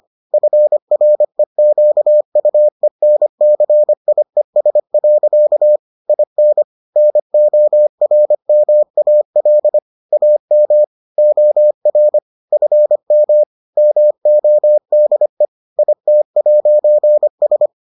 Morse code, heard as a keyed tone, about 25 wpm, 600 Hz. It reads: FREQUENCIES. IN NORMAL AM OR FM MODE IT'S